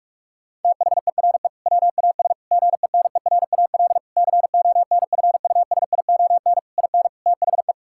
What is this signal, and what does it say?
Morse code, heard as a keyed tone, about 46 wpm, 700 Hz. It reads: THERE WAS GENERAL CONFUSION IN THE
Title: Sentences from Top 1600 Words 45wpm